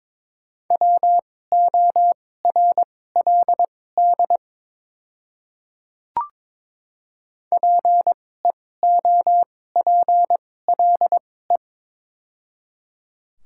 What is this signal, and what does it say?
Morse code, heard as a keyed tone, about 22 wpm, spaced as for 16 wpm, 700 Hz. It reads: WORLD PEOPLE
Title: 2x Character Spacing - Top 200 Words - 22wpm